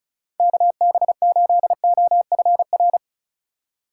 Morse code, heard as a keyed tone, about 34 wpm, 700 Hz. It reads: KB8OFR
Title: USA Call Signs - 6 Characters - 2x Character Spacing - 35wpm